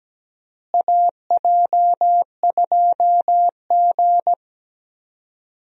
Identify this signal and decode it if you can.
Morse code, heard as a keyed tone, about 17 wpm, 700 Hz. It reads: AJ2G
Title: USA Call Signs - 4 Characters - Repeated 2x - 17wpm